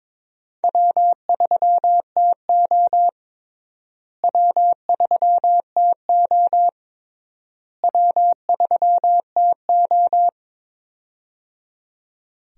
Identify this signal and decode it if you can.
Morse code, heard as a keyed tone, about 22 wpm, 700 Hz. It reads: W3TO W3TO W3TO
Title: USA Call Signs - 4 Characters - Repeated 3x - 22wpm